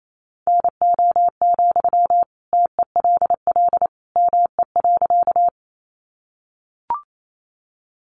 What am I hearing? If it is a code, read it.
Morse code, heard as a keyed tone, about 28 wpm, 700 Hz. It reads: NO, TELL ME.
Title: Sentences from Top 200 Words 28wpm